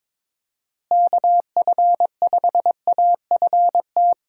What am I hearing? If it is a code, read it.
Morse code, heard as a keyed tone, about 21 wpm, 700 Hz. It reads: KF5AFT
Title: USA Call Signs - 6 Characters - Repeated 3x - 22wpm